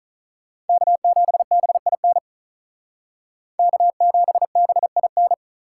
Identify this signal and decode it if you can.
Morse code, heard as a keyed tone, about 45 wpm, 700 Hz. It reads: K7BIN K7BIN